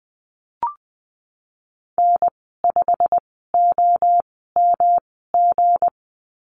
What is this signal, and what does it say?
Morse code, heard as a keyed tone, about 20 wpm, spaced as for 15 wpm, 700 Hz. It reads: N5OMG
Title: 173 Lesson 28g - Number 9 - Call Signs 20wpm